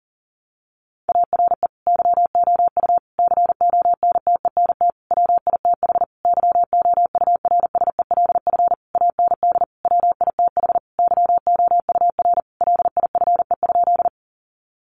Morse code, heard as a keyed tone, about 40 wpm, 700 Hz. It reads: ARE YOU CONTENT WITH YOURSELF AND WITH YOUR LIFE?
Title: Sentences from Top 900 Words 40wpm